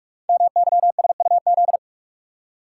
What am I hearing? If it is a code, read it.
Morse code, heard as a keyed tone, about 44 wpm, 700 Hz. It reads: MYSUZ
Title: Code Groups - 5 Letters - 45wpm